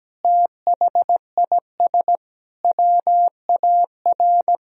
Morse code, heard as a keyed tone, about 17 wpm, 700 Hz. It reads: THIS WAR
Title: Sentences from Top 1000 Words - Review Part 2 - 17wpm